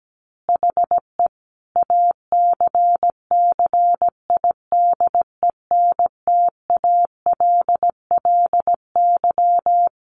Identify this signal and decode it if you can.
Morse code, heard as a keyed tone, about 17 wpm, 700 Hz. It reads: HE ACCIDENTALLY
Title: Additional Easy Phrases 17wpm